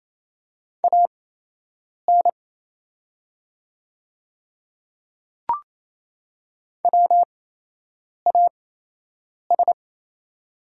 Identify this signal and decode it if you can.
Morse code, heard as a keyed tone, about 27 wpm, spaced as for 8 wpm, 700 Hz. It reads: AN WAS
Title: 8x Character Spacing - Top 100 Words - 28wpm